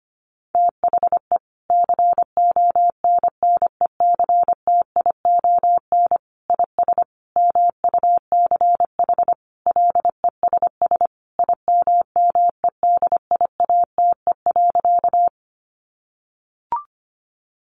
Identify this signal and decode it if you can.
Morse code, heard as a keyed tone, about 25 wpm, 700 Hz. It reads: THE CONNECTION IS MUCH LESS IMMEDIATE.
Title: Sentences from Top 1300 Words 25wpm